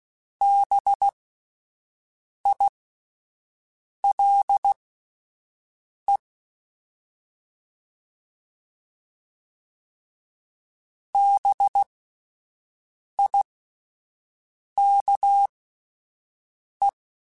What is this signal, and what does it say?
Morse code, heard as a keyed tone, about 16 wpm, spaced as for 5 wpm, 800 Hz. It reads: BILE BIKE
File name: Spaced Alphas / Spaced Alphas mp3s / Bs sp @10